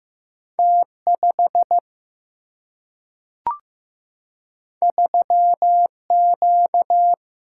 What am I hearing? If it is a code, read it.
Morse code, heard as a keyed tone, about 15 wpm, 700 Hz. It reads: T5 3Q
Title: Single-Letter Single-Number 15wpm